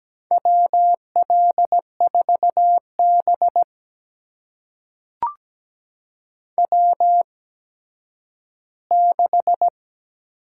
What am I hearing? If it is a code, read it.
Morse code, heard as a keyed tone, about 17 wpm, spaced as for 10 wpm, 700 Hz. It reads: WL4B W6
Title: USA Call Signs - 4 Characters - 8x Character Spacing - 17wpm